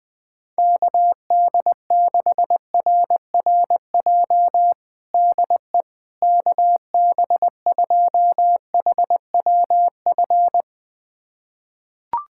Morse code, heard as a keyed tone, about 21 wpm, 700 Hz. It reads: KD6RRJ DE KB2HWF
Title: USA Callsign DE Callsign - 6 Characters - 20wpm